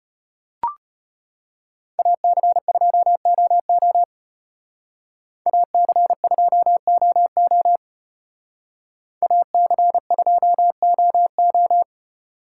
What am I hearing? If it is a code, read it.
Morse code, heard as a keyed tone, about 31 wpm, 700 Hz. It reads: AC2OO AC2OO AC2OO
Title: Speed Racing - USA 2x2 Call Signs 25wpm